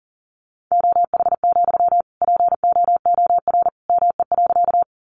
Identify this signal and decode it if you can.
Morse code, heard as a keyed tone, about 40 wpm, 700 Hz. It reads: OH, POOR ME.
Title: Sentences from Top 1000 Words 40wpm